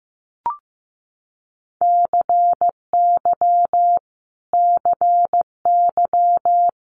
Morse code, heard as a keyed tone, about 15 wpm, 700 Hz. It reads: CY CY
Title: Multi-letter Phonemes Repeated 3x 15wpm